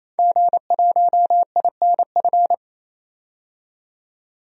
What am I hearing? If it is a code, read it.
Morse code, heard as a keyed tone, about 28 wpm, 700 Hz. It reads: G1INF
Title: DXCC-100 Call Signs - 5 Characters - 28wpm